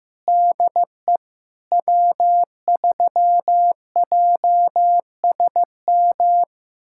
Morse code, heard as a keyed tone, about 15 wpm, 700 Hz. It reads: DE W3JSM
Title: USA Callsign DE Callsign - 5 Characters - 15wpm